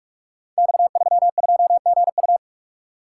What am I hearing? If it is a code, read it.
Morse code, heard as a keyed tone, about 44 wpm, 700 Hz. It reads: X32GU
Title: Code Groups - 5 Letter:Numbers - 45wpm